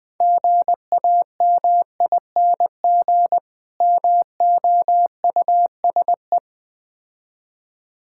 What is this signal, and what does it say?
Morse code, heard as a keyed tone, about 20 wpm, 700 Hz. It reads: GAMING MOUSE
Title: Difficult Words with Hints - 20wpm